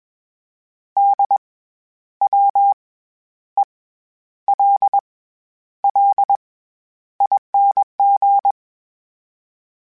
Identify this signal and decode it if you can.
Morse code, heard as a keyed tone, about 21 wpm, spaced as for 10 wpm, 800 Hz. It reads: DWELLING
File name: Special INGs-mp3s / Special INGs-mp3s / SlowToFaster ing @20